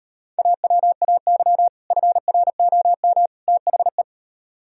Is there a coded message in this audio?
Yes: Morse code from 0.4 to 4.0 s, heard as a keyed tone, about 38 wpm, 700 Hz. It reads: AWAY FROM THE